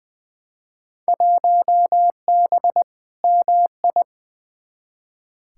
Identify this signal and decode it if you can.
Morse code, heard as a keyed tone, about 20 wpm, 700 Hz. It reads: 1B MI
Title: Field Day Exchanges - 20wpm